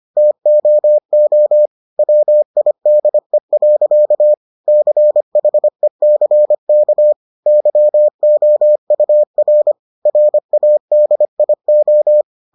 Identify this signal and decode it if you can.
Morse code, heard as a keyed tone, about 25 wpm, 600 Hz. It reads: TOO WIDE. CHECK YOUR RADIO